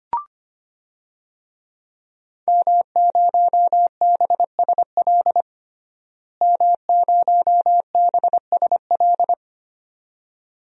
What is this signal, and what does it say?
Morse code, heard as a keyed tone, about 25 wpm, 700 Hz. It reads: M0BSL M0BSL
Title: DXCC-100 Call Signs - 5 Characters - Repeated 2x - 25wpm